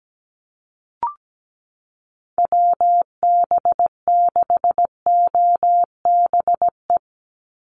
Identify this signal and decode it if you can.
Morse code, heard as a keyed tone, about 17 wpm, 700 Hz. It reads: WB6OBE
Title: USA Call Signs 17wpm